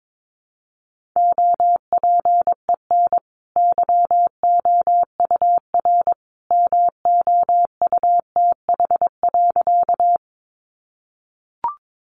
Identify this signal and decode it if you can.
Morse code, heard as a keyed tone, about 22 wpm, 700 Hz. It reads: OPEN YOUR MOUTH.
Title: Sentences from Top 1500 Words 22wpm